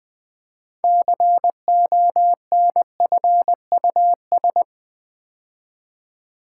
Morse code, heard as a keyed tone, about 20 wpm, 700 Hz. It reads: CONFUS
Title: Build A Word - 3 Letters At A Time - 8-Letter Words - 20wpm